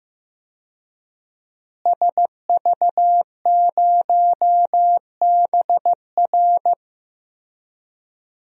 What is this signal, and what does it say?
Morse code, heard as a keyed tone, about 15 wpm, 700 Hz. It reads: SV0BR
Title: DXCC-100 Call Signs - 5 Characters - Repeated 3x - 15wpm